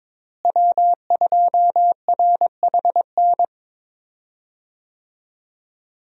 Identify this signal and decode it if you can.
Morse code, heard as a keyed tone, about 22 wpm, 700 Hz. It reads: W2RHN
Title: USA Call Signs - 5 Characters - Repeated 2x - 22wpm